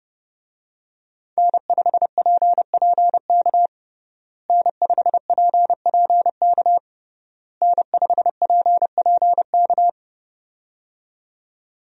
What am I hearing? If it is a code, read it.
Morse code, heard as a keyed tone, about 32 wpm, 700 Hz. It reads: N5PPK N5PPK N5PPK
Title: USA Call Signs - 5 Characters - Repeated 3x - 30wpm